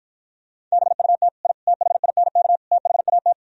Morse code, heard as a keyed tone, about 53 wpm, 700 Hz. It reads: BUT I THINK THAT